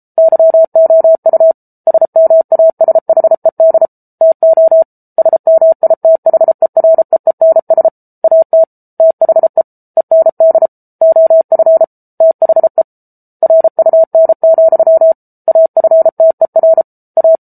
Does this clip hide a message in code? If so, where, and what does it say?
Yes: Morse code from 0.2 to 17.4 s, heard as a keyed tone, about 34 wpm, 650 Hz. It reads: YOU SMASHED TO SMITHEREENS AT THE END OF THE RUN, AFTER A